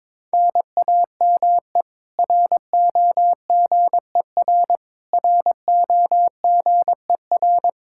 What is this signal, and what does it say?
Morse code, heard as a keyed tone, about 22 wpm, 700 Hz. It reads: NAME ROGER ROGER
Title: Practice QSO - Exchange 1 - Beginning - 22wpm